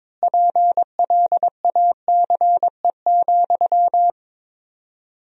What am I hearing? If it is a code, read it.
Morse code, heard as a keyed tone, about 22 wpm, 700 Hz. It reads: PLACE,
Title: 4x Word-Spacing - Sentences from Top 400 Words - 22wpm